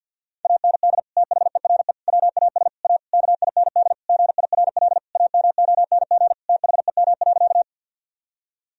Morse code, heard as a keyed tone, about 50 wpm, 700 Hz. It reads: AND THERE WAS A KIND GIRL AMONG THEM.
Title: Sentences from Top 1000 Words - Review Part 3 - 50wpm